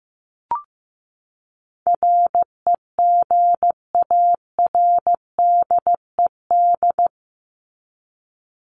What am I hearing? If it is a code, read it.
Morse code, heard as a keyed tone, about 15 wpm, 700 Hz. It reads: REGARDED
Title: ED Suffix - Top 100 Words with Base Words 15wpm